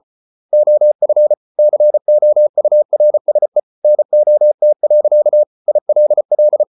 Morse code, heard as a keyed tone, about 34 wpm, 600 Hz. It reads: OF COURSE NOT. ILL